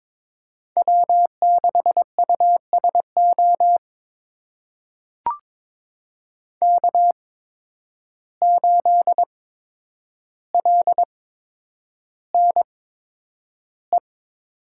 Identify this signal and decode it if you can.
Morse code, heard as a keyed tone, about 22 wpm, spaced as for 9 wpm, 700 Hz. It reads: W6USO K8LNE
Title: USA Call Signs - 5 Characters - 8x Character Spacing - 22wpm